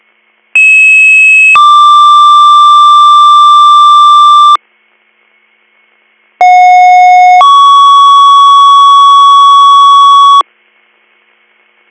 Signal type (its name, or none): two-tone page